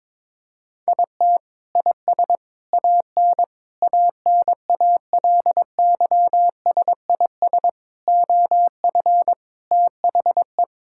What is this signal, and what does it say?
Morse code, heard as a keyed tone, about 22 wpm, 700 Hz. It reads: IT IS AN ANALYSIS OF THE
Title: Sentences from Top 1000 Words - Review Part 5 - 22wpm